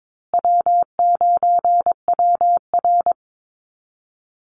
Morse code, heard as a keyed tone, about 22 wpm, 700 Hz. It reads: W9WR
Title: Call Signs Repeated 2x - 22wpm